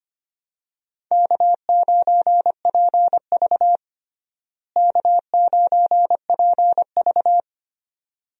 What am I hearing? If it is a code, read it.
Morse code, heard as a keyed tone, about 25 wpm, 700 Hz. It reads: K9PV K9PV
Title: USA Call Signs - 4 Characters - Repeated 2x - 25wpm